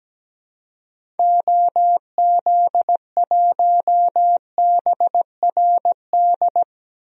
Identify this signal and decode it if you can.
Morse code, heard as a keyed tone, about 17 wpm, 700 Hz. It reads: OZ1BRD
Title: DXCC-100 Call Signs - 6 Characters - Repeated 3x - 17wpm